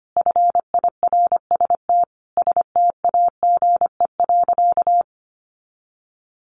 Morse code, heard as a keyed tone, about 25 wpm, 700 Hz. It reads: FIRST STAGE.
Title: Sentences from Top 1200 Words 25wpm